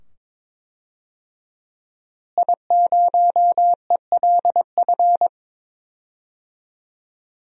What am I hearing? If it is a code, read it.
Morse code, heard as a keyed tone, about 22 wpm, 700 Hz. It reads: I0ELF
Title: DXCC-100 Call Signs - 5 Characters - Repeated 2x - 22wpm